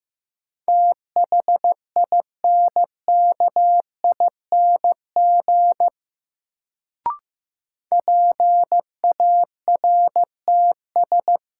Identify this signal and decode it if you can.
Morse code, heard as a keyed tone, about 16 wpm, 700 Hz. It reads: THINKING PARTS
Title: Sentences from Top 1000 Words - Review Part 6 - New Word Drill - 15wpm